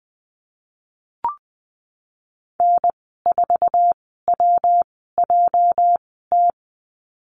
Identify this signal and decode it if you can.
Morse code, heard as a keyed tone, about 19 wpm, spaced as for 14 wpm, 700 Hz. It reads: N4WJT
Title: 213 Lesson 34g - Number 8 - Call Signs 20wpm